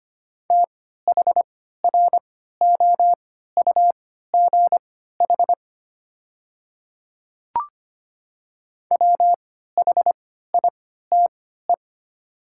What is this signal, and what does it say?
Morse code, heard as a keyed tone, about 25 wpm, spaced as for 14 wpm, 700 Hz. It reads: THROUGH WHITE
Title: 3x Character Spacing - Top 300 Words - 25wpm